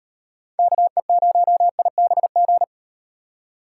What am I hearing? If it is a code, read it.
Morse code, heard as a keyed tone, about 38 wpm, 700 Hz. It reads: KE0IBG